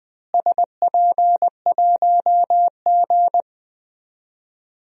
Morse code, heard as a keyed tone, about 20 wpm, 700 Hz. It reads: SP1G
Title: DXCC-100 Call Signs - 4 Characters - Repeated 2x - 20wpm